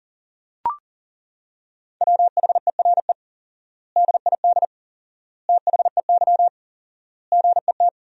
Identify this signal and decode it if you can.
Morse code, heard as a keyed tone, about 40 wpm, 700 Hz. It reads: WHERE DID THEY GET
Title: 4x Word-Spacing - Sentences from Top 400 Words - 40wpm